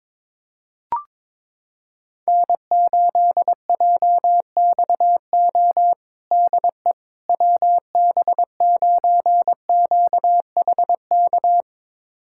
Morse code, heard as a keyed tone, about 22 wpm, 700 Hz. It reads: N8JXO DE WB9QHK
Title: USA Callsign DE Callsign 22wpm